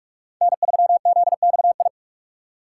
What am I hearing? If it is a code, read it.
Morse code, heard as a keyed tone, about 47 wpm, 700 Hz. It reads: N3ZXI
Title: USA Call Signs - 5 Characters - 2x Character Spacing - 45wpm